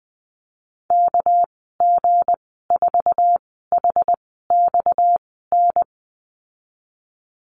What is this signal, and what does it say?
Morse code, heard as a keyed tone, about 19 wpm, 700 Hz. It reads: K G 4 H X N
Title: 238 Lesson 38g - Letter q - Call Signs 20wpm